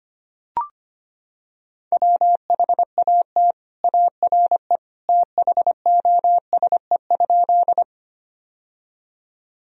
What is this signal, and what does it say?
Morse code, heard as a keyed tone, about 25 wpm, 700 Hz. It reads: WHAT ARE THOSE?
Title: Sentences from Top 1000 Words - Review Part 1 - 25wpm